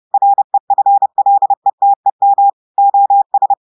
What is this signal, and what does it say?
Morse code, heard as a keyed tone, about 30 wpm, 800 Hz. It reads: REFLETEM OS
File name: audios / portugais / portugais texte en clair 1 vitesse 30 mots minutes